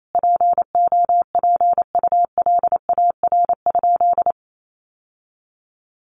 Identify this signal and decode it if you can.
Morse code, heard as a keyed tone, about 28 wpm, 700 Hz. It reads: POPULAR?
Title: Sentences from Top 1200 Words 28wpm